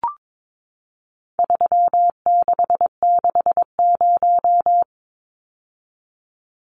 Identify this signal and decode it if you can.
Morse code, heard as a keyed tone, about 22 wpm, 700 Hz. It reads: 3660
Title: Numbers - 4-Digit 22wpm